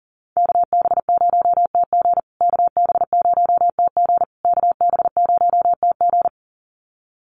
Morse code, heard as a keyed tone, about 38 wpm, 700 Hz. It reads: KB0TG KB0TG KB0TG
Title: USA Call Signs Repeated 3x 40wpm